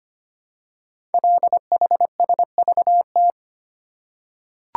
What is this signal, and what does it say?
Morse code, heard as a keyed tone, about 25 wpm, 700 Hz. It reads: LHSVT
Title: Code Groups - 5 Letters - 25wpm